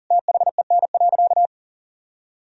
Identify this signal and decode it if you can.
Morse code, heard as a keyed tone, about 40 wpm, 700 Hz. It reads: THEN.
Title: Sentences from Top 1000 Words - Review Part 3 - 40wpm